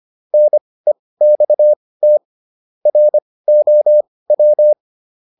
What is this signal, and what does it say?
Morse code, heard as a keyed tone, about 25 wpm, spaced as for 18 wpm, 600 Hz. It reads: NEXT ROW